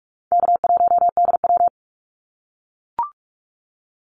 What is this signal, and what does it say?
Morse code, heard as a keyed tone, about 44 wpm, 700 Hz. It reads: K1DW
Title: Easy Call Signs - 45wpm